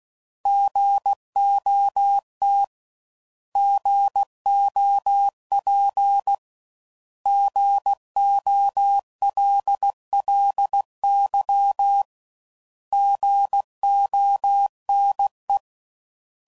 Morse code, heard as a keyed tone, about 16 wpm, 800 Hz. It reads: GOT GOP GOLLY GONE
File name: Prefixes / By First Letter / Gs